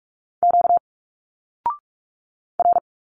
Morse code, heard as a keyed tone, about 46 wpm, 700 Hz. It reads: Q R
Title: Single Letters 45wpm